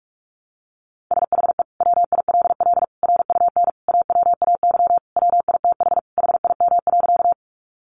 Morse code, heard as a keyed tone, about 44 wpm, 700 Hz. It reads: SHE WILL RUN AWAY WITH HIM.
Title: Sentences from Top 600 Words 45wpm